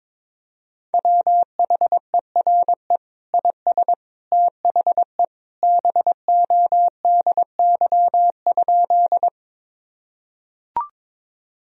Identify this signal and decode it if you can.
Morse code, heard as a keyed tone, about 22 wpm, 700 Hz. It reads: WHERE IS THE BODY?
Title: Sentences from Top 1000 Words - Review Part 2 - 22wpm